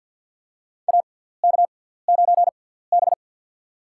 Morse code, heard as a keyed tone, about 46 wpm, spaced as for 17 wpm, 700 Hz. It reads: AK9B
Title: USA Call Signs - 4 Characters - 6x Character Spacing - 50wpm